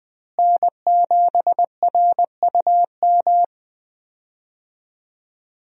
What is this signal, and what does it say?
Morse code, heard as a keyed tone, about 20 wpm, 700 Hz. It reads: N7RUM
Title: USA Call Signs - 5 Characters - Repeated 2x - 20wpm